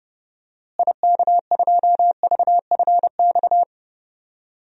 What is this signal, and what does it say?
Morse code, heard as a keyed tone, about 30 wpm, 700 Hz. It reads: IK2VFX
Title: DXCC-100 Call Signs - 6 Characters - Repeated 2x - 30wpm